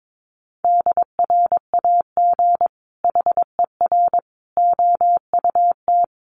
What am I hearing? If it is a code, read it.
Morse code, heard as a keyed tone, about 22 wpm, 700 Hz. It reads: DRAG HER OUT
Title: Sentences from Top 1500 Words 22wpm